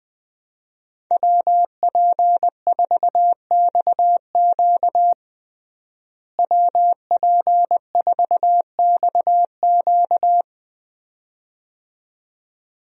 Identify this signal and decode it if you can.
Morse code, heard as a keyed tone, about 21 wpm, 700 Hz. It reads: WP4XQ WP4XQ
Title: USA Call Signs - 5 Characters - Repeated 2x - 20wpm